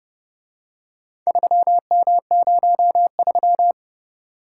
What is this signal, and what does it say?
Morse code, heard as a keyed tone, about 27 wpm, 700 Hz. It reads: 3M03